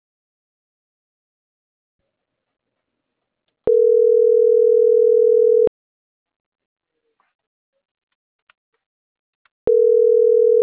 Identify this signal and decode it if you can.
Ringback tone (North America), 440 + 480 Hz.